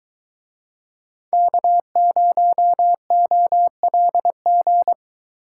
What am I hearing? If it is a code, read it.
Morse code, heard as a keyed tone, about 22 wpm, 700 Hz. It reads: K0OLG